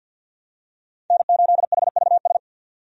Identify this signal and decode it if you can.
Morse code, heard as a keyed tone, about 52 wpm, 700 Hz. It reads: N8HVS
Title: USA Call Signs - 5 Characters - 50wpm